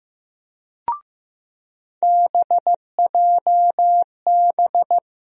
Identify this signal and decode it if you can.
Morse code, heard as a keyed tone, about 15 wpm, 700 Hz. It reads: BJB
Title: USA Call Sign Postfixes - 3 Letters 15wpm